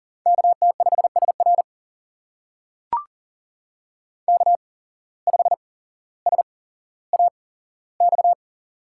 Morse code, heard as a keyed tone, about 39 wpm, spaced as for 17 wpm, 700 Hz. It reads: KT5SR K5SAX
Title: USA Call Signs - 5 Characters - 8x Character Spacing - 40wpm